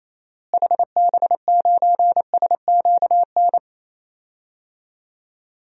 Morse code, heard as a keyed tone, about 28 wpm, 700 Hz. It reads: HB9SQN
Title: DXCC-100 Call Signs - 6 Characters - Repeated 2x - 28wpm